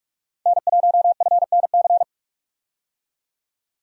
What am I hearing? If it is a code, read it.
Morse code, heard as a keyed tone, about 46 wpm, 700 Hz. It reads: N1FNC
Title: USA Call Signs - 5 Characters - 45wpm